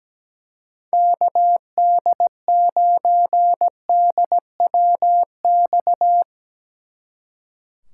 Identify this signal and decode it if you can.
Morse code, heard as a keyed tone, about 17 wpm, 700 Hz. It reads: KD9DWX
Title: Speed Racing - USA 2x3 Call Signs 17wpm